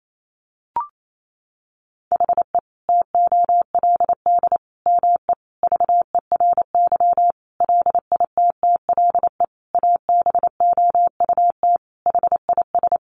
Morse code, heard as a keyed tone, about 28 wpm, 700 Hz. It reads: HE TOLD ME VERY LITTLE ABOUT HIS
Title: Sentences from Top 1500 Words 28wpm